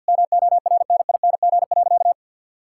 Morse code, heard as a keyed tone, about 50 wpm, 700 Hz. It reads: MORNING.
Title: Sentences from Top 1000 Words - Review Part 5 - 50wpm